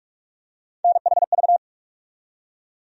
Morse code, heard as a keyed tone, about 42 wpm, 700 Hz. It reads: NHV
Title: USA Call Sign Postfixes - 3 Letters 45wpm